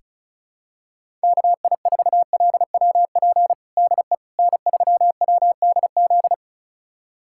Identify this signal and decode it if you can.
Morse code, heard as a keyed tone, about 34 wpm, 700 Hz. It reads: KI4LWP DE N3WDZ
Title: USA Callsign DE Callsign 35wpm